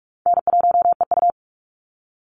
Morse code, heard as a keyed tone, about 44 wpm, 700 Hz. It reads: N1EU
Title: NA SOTA Activator Callsigns 45wpm